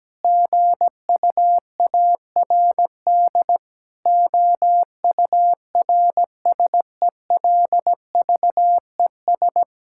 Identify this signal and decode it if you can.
Morse code, heard as a keyed tone, about 17 wpm, 700 Hz. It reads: GUARD OURSELVES
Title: Sentences from Top 1600 Words 17wpm